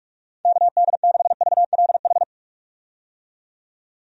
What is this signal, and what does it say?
Morse code, heard as a keyed tone, about 46 wpm, 700 Hz. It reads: KD6VLH
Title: USA Call Signs - 6 Characters - 45wpm